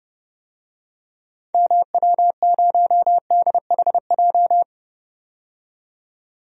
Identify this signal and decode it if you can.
Morse code, heard as a keyed tone, about 30 wpm, 700 Hz. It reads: MW0DHJ
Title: DXCC-100 Call Signs - 6 Characters - 30wpm